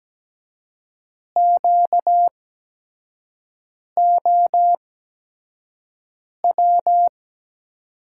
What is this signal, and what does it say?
Morse code, heard as a keyed tone, about 17 wpm, spaced as for 5 wpm, 700 Hz. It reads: QOW
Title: USA Call Signs - 5 Characters - 8x Character Spacing - 17wpm